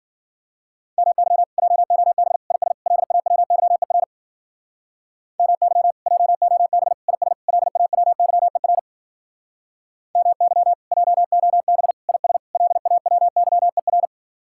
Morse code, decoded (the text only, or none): MY JOB IS LAWYER MY JOB IS LAWYER MY JOB IS LAWYER